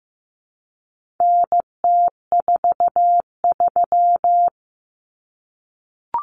Morse code, heard as a keyed tone, about 15 wpm, 700 Hz. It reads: NT43
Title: Common Cut Numbers - 4-Digit 15wpm